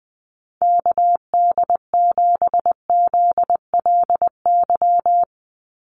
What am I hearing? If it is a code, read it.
Morse code, heard as a keyed tone, about 20 wpm, 700 Hz. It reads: KD7ZLY
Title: USA Call Signs 20wpm